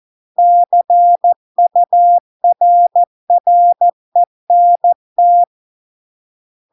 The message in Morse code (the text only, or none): CURRENT